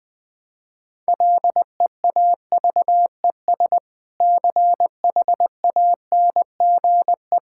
Morse code, heard as a keyed tone, about 20 wpm, 700 Hz. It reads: LEAVES CHANGE
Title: Difficult Sentences with Hints - 20wpm